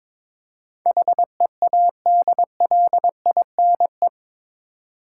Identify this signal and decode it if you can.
Morse code, heard as a keyed tone, about 22 wpm, 700 Hz. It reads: HEADLINE
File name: practice / 22wpm / Eight Letter Words - 22wpm